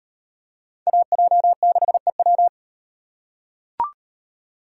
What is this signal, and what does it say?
Morse code, heard as a keyed tone, about 38 wpm, 700 Hz. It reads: AJ6EW